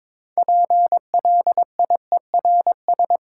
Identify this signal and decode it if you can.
Morse code, heard as a keyed tone, about 22 wpm, 700 Hz. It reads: PLIERS
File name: Difficult Words with Hints - 22wpm